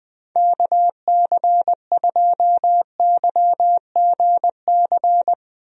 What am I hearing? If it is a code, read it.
Morse code, heard as a keyed tone, about 20 wpm, 700 Hz. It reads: KC2YGC
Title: USA Call Signs - 6 Characters - 20wpm